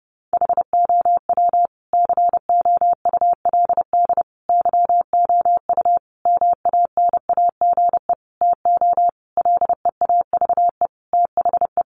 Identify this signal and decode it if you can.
Morse code, heard as a keyed tone, about 30 wpm, 700 Hz. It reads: HOW COULD YOU MANAGE TO LEAVE THE
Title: Sentences from Top 1400 Words 30wpm